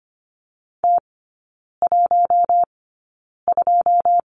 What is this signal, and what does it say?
Morse code, heard as a keyed tone, about 25 wpm, spaced as for 9 wpm, 690 Hz. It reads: T12
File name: Lesson 14e - Number 2 - Three Characters 25wpm